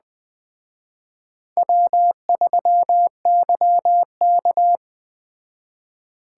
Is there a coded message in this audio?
Yes: Morse code from 1.6 to 4.8 s, heard as a keyed tone, about 19 wpm, 700 Hz. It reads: W3YK